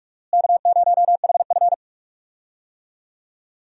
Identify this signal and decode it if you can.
Morse code, heard as a keyed tone, about 46 wpm, 700 Hz. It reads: K0HF
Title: USA Call Signs - 4 Characters - Repeated 2x - 45wpm